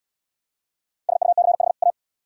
Morse code, heard as a keyed tone, about 69 wpm, 700 Hz. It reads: SUCH A